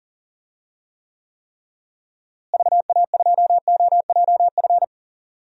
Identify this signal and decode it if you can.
Morse code, heard as a keyed tone, about 40 wpm, 700 Hz. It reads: VA2OJF
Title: DXCC-100 Call Signs - 6 Characters - Repeated 3x - 40wpm